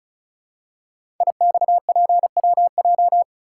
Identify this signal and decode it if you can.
Morse code, heard as a keyed tone, about 35 wpm, 690 Hz. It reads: IXPWJ
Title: Code Groups - 5 Letters - 35wpm